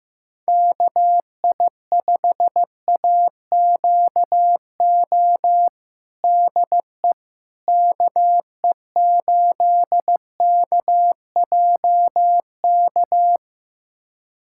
Morse code, heard as a keyed tone, about 15 wpm, 700 Hz. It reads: KI5AQO DE KE8KJK
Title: USA Callsign DE Callsign 15wpm